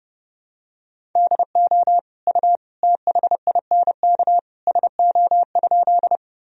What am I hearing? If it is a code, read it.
Morse code, heard as a keyed tone, about 30 wpm, 700 Hz. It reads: DO U THINK SO?
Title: Sentences from Top 200 Words and Abbreviations 30wpm